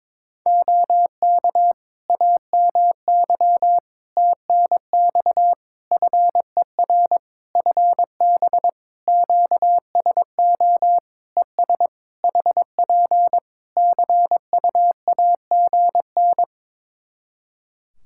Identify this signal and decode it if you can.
Morse code, heard as a keyed tone, about 22 wpm, 700 Hz. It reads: OK AMY TNX FER FB QSO ES HP CUAGN
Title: Practice QSO - Exchange 4 - Ending - 22wpm